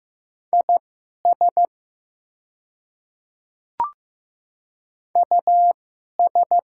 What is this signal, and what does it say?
Morse code, heard as a keyed tone, about 15 wpm, spaced as for 11 wpm, 700 Hz. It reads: IS US
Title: 2x Character Spacing - Top 300 Words - 15wpm